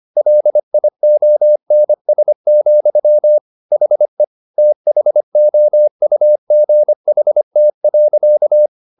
Morse code, heard as a keyed tone, about 25 wpm, 600 Hz. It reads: LIONS, HE THOUGHT.
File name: the old man and the sea part 1 25wpm